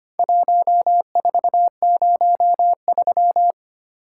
Morse code, heard as a keyed tone, about 25 wpm, 700 Hz. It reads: 1403
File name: practice / 25wpm / Four Letter-Number Combinations - 25wpm